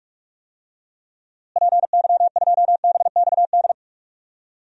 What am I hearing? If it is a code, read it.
Morse code, heard as a keyed tone, about 46 wpm, 700 Hz. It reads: PY2BXD
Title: DXCC-100 Call Signs - 6 Characters - Repeated 2x - 45wpm